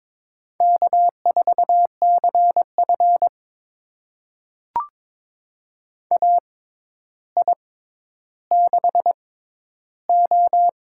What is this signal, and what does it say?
Morse code, heard as a keyed tone, about 22 wpm, spaced as for 11 wpm, 700 Hz. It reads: K4CF AI6O
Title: USA Call Signs - 6x Character Spacing - 22wpm